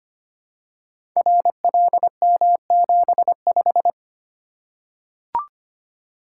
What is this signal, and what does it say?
Morse code, heard as a keyed tone, about 25 wpm, 700 Hz. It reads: RLM75
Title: Code Groups - 5 Letter:Numbers - 25wpm